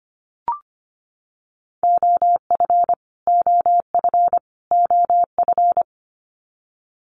Morse code, heard as a keyed tone, about 25 wpm, 700 Hz. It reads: OF OF OF
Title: Top 500 Words Repeated 25wpm